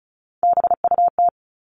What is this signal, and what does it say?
Morse code, heard as a keyed tone, about 34 wpm, 700 Hz. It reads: BUT
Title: Top 500 Words 35wpm